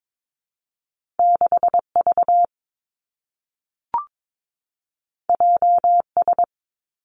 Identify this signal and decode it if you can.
Morse code, heard as a keyed tone, about 22 wpm, 700 Hz. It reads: 6 V J S
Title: Two Challenging Letter-Number Combinations 22wpm